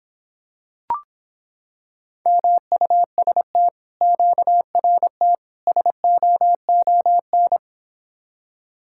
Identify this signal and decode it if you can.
Morse code, heard as a keyed tone, about 26 wpm, 700 Hz. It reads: MUST QRT SOON